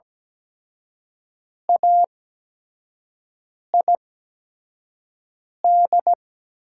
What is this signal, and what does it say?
Morse code, heard as a keyed tone, about 17 wpm, spaced as for 5 wpm, 700 Hz. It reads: AID
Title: 8x Character Spacing - Top 200 Words - 17wpm